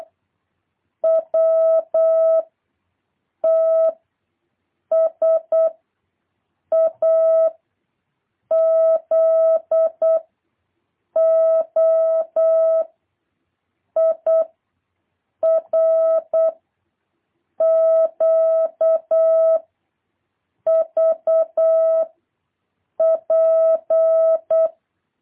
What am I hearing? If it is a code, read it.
Morse code, heard as a keyed tone, about 8 wpm, 650 Hz. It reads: W T S A Z O I R Q V P
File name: jh8cbh / 2023cw